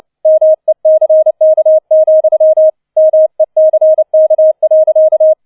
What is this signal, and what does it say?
Morse code, heard as a keyed tone, about 29 wpm, 630 Hz. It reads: MECK, MECK.